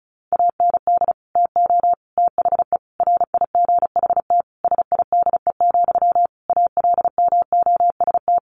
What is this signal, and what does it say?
Morse code, heard as a keyed tone, about 34 wpm, 700 Hz. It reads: AND TO THE RIGHT SIDE, ALMOST
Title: Sentences from Top 1300 Words 35wpm